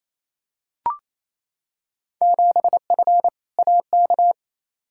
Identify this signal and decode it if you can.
Morse code, heard as a keyed tone, about 28 wpm, 700 Hz. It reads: 7F AK
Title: Field Day Exchanges Encore - 28wpm